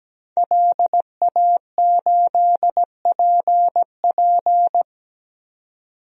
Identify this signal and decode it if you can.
Morse code, heard as a keyed tone, about 17 wpm, 700 Hz. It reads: LA8PP
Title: DXCC-100 Call Signs - 5 Characters - 17wpm